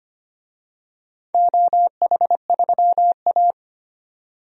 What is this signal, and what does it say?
Morse code, heard as a keyed tone, about 25 wpm, 700 Hz. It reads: OH3A
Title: DXCC-100 Call Signs - 4 Characters - 25wpm